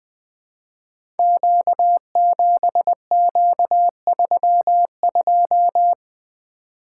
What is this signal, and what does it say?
Morse code, heard as a keyed tone, about 21 wpm, 700 Hz. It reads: Q7Q32
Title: Code Groups - 5 Letter:Numbers - 20wpm